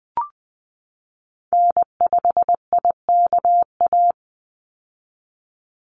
Morse code, heard as a keyed tone, about 20 wpm, 700 Hz. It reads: N5IKA
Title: USA Call Signs 20wpm